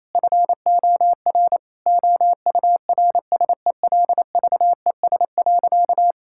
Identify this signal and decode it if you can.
Morse code, heard as a keyed tone, about 28 wpm, 700 Hz. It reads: FOR OURSELVES.
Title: Sentences from Top 1600 Words 28wpm